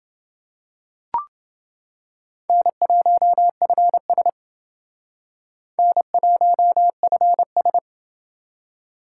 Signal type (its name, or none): Morse code